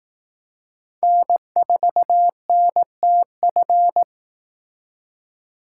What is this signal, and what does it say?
Morse code, heard as a keyed tone, about 18 wpm, 700 Hz. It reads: N4NTF
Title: Speed Racing - USA 1x3 Call Signs 15wpm